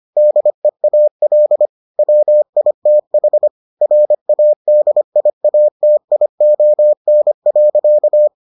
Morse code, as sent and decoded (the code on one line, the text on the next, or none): -.. .- .-.. / .-- .. - .... / .-. .- -.. .. .- - .. --- -. .-.-.-
DEAL WITH RADIATION.